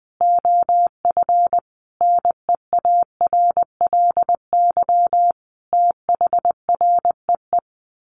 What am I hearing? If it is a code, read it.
Morse code, heard as a keyed tone, about 20 wpm, 700 Hz. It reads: OF NEARLY THREE
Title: Sentences from Top 1300 Words 20wpm